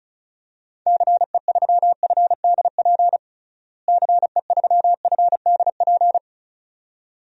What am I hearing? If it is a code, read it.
Morse code, heard as a keyed tone, about 34 wpm, 700 Hz. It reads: CE3FDP CE3FDP
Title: DXCC-100 Call Signs - 6 Characters - Repeated 2x - 35wpm